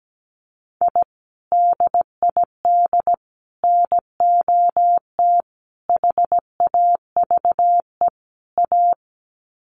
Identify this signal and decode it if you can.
Morse code, heard as a keyed tone, about 17 wpm, 700 Hz. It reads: I DID NOT HAVE A
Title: Sets of 5 Words 17wpm